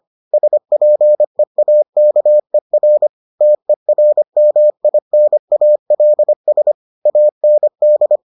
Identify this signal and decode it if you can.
Morse code, heard as a keyed tone, about 25 wpm, 600 Hz. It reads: SPEAKER TERMINALS AND